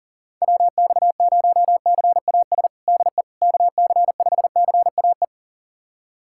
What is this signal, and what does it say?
Morse code, heard as a keyed tone, about 40 wpm, 700 Hz. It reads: WX0CAS DE KC5CAE
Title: USA Callsign DE Callsign - 6 Characters - 40wpm